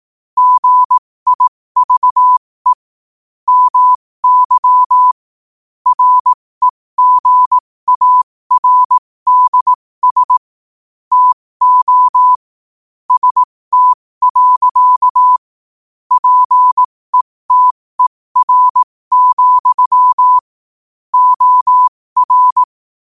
Morse code, heard as a keyed tone, about 18 wpm, spaced as for 15 wpm, 1000 Hz. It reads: GIVE MY REGARDS TO ST. PETER, OR
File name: qotdfiles / 15wpm